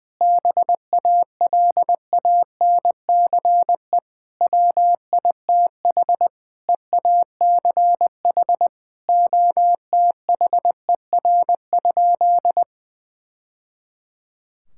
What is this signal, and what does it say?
Morse code, heard as a keyed tone, about 20 wpm, 700 Hz. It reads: BALANCE WITH EACH OTHER?
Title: Sentences from Top 1000 Words - Review Part 6 - 20wpm